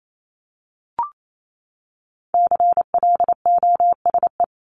Morse code, heard as a keyed tone, about 28 wpm, 700 Hz. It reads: CLOSE
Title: LY Suffix - Top 100 Words with Base Words 28wpm